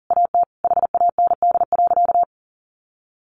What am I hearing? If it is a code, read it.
Morse code, heard as a keyed tone, about 40 wpm, 700 Hz. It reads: AT HAND.
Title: Sentences from Top 1500 Words 40wpm